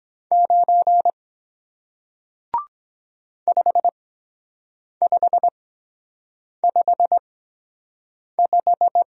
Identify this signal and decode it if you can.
Morse code, heard as a keyed tone, about 23 wpm, spaced as for 7 wpm, 700 Hz. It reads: 9 5555